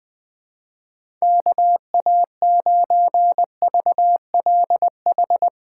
Morse code, heard as a keyed tone, about 20 wpm, 700 Hz. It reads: KA9VLH
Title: USA Callsign DE Callsign - 6 Characters - 20wpm